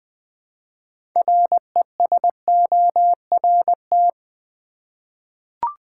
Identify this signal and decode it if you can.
Morse code, heard as a keyed tone, about 20 wpm, 700 Hz. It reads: RESORT
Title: 6 Letter Words - First 3 Letters Given - 20wpm